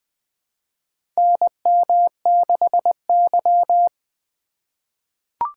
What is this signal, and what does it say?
Morse code, heard as a keyed tone, about 21 wpm, 700 Hz. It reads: NM6Y
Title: USA Call Signs - 4 Characters - 8x Character Spacing - 20wpm